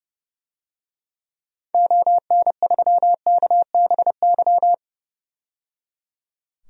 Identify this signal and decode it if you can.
Morse code, heard as a keyed tone, about 30 wpm, 700 Hz. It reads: ON3KBY
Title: DXCC-100 Call Signs - 6 Characters - 30wpm